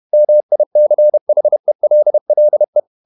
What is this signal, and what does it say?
Morse code, heard as a keyed tone, about 31 wpm, 600 Hz. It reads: MICHELLE
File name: Ferie QRQ 31wpm